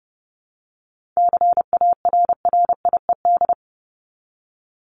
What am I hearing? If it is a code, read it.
Morse code, heard as a keyed tone, about 30 wpm, 700 Hz. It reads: CARRIED
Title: ED Suffix - Top 100 Words with Base Words 30wpm